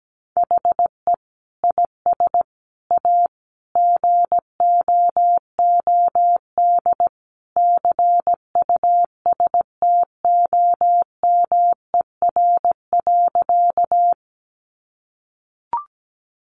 Morse code, heard as a keyed tone, about 17 wpm, 700 Hz. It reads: HE IS A GOOD CUSTOMER.
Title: Sentences from Top 1100 Words 17wpm